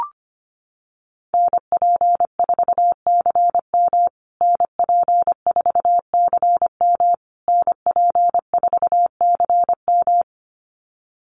Morse code, heard as a keyed tone, about 25 wpm, 700 Hz. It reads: NP4CM NP4CM NP4CM
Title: USA Call Signs Repeated 3x 25wpm